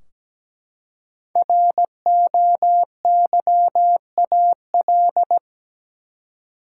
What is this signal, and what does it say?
Morse code, heard as a keyed tone, about 17 wpm, 700 Hz. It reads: ROYAL
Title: Five Letter Words - 17wpm